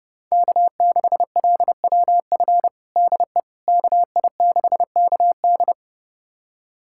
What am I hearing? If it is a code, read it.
Morse code, heard as a keyed tone, about 30 wpm, 700 Hz. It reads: K6LWF DE KI6KD
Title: USA Callsign DE Callsign - 5 Characters - 30wpm